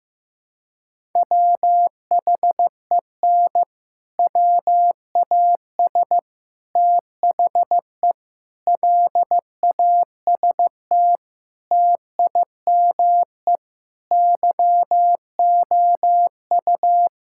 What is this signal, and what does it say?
Morse code, heard as a keyed tone, about 15 wpm, 700 Hz. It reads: WHEN WAS THE LAST TIME YOU
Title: Sentences from Top 1000 Words - Review Part 5 - 15wpm